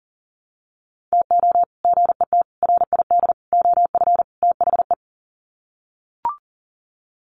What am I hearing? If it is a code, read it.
Morse code, heard as a keyed tone, about 40 wpm, 700 Hz. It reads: TO GET RID OF THE
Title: Sets of 5 Words 40wpm